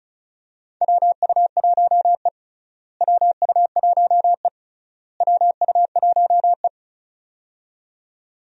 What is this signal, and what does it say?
Morse code, heard as a keyed tone, about 35 wpm, 700 Hz. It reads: WU1E WU1E WU1E
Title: USA Call Signs - 4 Characters - Repeated 3x - 35wpm